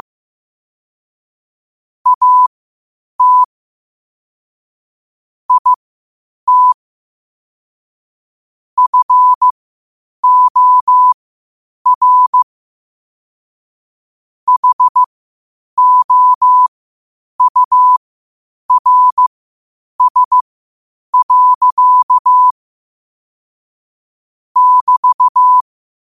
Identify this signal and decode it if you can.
Morse code, heard as a keyed tone, about 15 wpm, spaced as for 8 wpm, 1000 Hz. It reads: AT IT FOR HOURS. <BT>
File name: qotdfiles / 8wpm